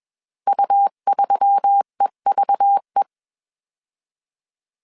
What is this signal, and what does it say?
Morse code, heard as a keyed tone, about 21 wpm, 800 Hz. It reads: U3EVE
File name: CW / DET / 1 9 1500C S B 20wpm 2313